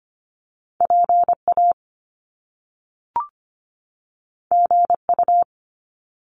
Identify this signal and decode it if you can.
Morse code, heard as a keyed tone, about 25 wpm, 700 Hz. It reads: PA GU
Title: Top Two Letter Combinations in Words 25wpm